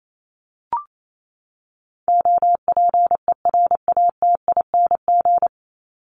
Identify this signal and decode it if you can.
Morse code, heard as a keyed tone, about 28 wpm, 700 Hz. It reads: OPERATING
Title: ING Suffix - Top 100 Words with Base Words 28wpm